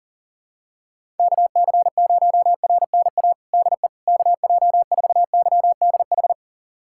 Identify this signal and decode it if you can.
Morse code, heard as a keyed tone, about 40 wpm, 700 Hz. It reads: KC0RNA DE KJ4YDH
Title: USA Callsign DE Callsign 40wpm